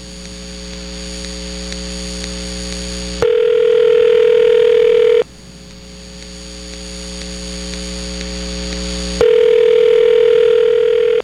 Ringback tone (North America), 440 + 480 Hz.